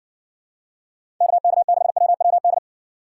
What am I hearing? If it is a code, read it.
Morse code, heard as a keyed tone, about 54 wpm, 700 Hz. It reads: KK6WWD